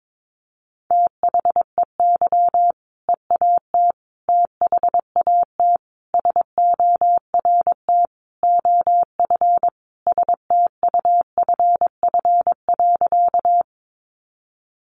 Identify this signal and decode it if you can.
Morse code, heard as a keyed tone, about 22 wpm, 700 Hz. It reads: THEY EAT THAT SORT OF STUFF.
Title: Sentences from Top 1400 Words 22wpm